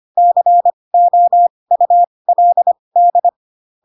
Morse code, heard as a keyed tone, about 25 wpm, spaced as for 20 wpm, 700 Hz. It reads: COULD
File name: Pinocchio / Pinocchio ch2 15 25 ik1hge